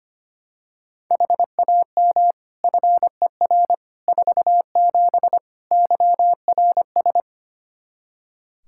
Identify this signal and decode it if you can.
Morse code, heard as a keyed tone, about 25 wpm, 700 Hz. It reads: HAM FER 47 YRS
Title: K1USN SST QSO Practice 25wpm